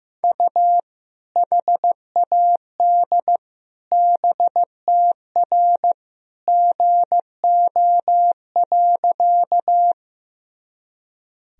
Morse code, heard as a keyed tone, about 15 wpm, 700 Hz. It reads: U HAD BTR GO.
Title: Sentences from Top 400 Words and Abbreviations 15wpm